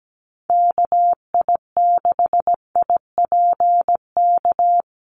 Morse code, heard as a keyed tone, about 17 wpm, 700 Hz. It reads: KI6IPK
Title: USA Call Signs Repeated 2x 17wpm